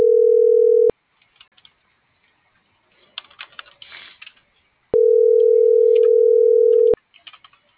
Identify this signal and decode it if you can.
Ringback tone (North America), 440 + 480 Hz.
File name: CENTRAL MABE / out xquenda.trinidad - .63514